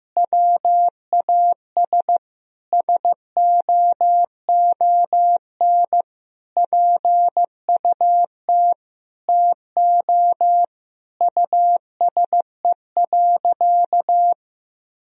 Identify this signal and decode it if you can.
Morse code, heard as a keyed tone, about 15 wpm, 700 Hz. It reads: WAS SOON PUT TO USE.
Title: Sentences from Top 1000 Words - Review Part 3 - 15wpm